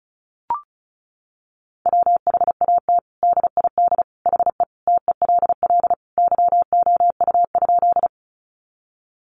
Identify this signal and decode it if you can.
Morse code, heard as a keyed tone, about 34 wpm, 700 Hz. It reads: WHAT DID HE TELL YOU?
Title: Sentences from Top 200 Words 35wpm